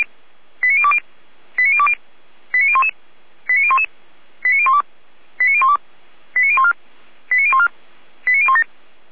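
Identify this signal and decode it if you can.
Five-tone selective call (ZVEI), heard as quick runs of five tones, tones of about 70 ms. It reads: 89020 89020 89011 89011 89012 89012 89014 89014 89017
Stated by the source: ZVEI five-tone selective call 89001 received by SDR, from Feuerwehr Orlamünde.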